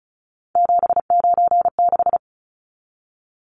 Morse code, heard as a keyed tone, about 34 wpm, 700 Hz. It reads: 796
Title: Numbers - 3-Digit 35wpm